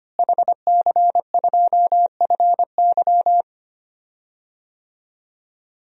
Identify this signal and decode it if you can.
Morse code, heard as a keyed tone, about 25 wpm, 700 Hz. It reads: HC2FY
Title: DXCC-100 Call Signs - 5 Characters - Repeated 2x - 25wpm